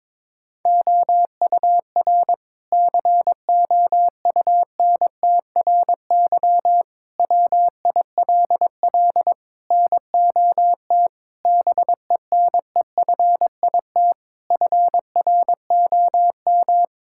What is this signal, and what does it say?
Morse code, heard as a keyed tone, about 22 wpm, 700 Hz. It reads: OUR COUNTRY WILL NOT BENEFIT FROM
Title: Sentences from Top 1000 Words - Review Part 6 - 22wpm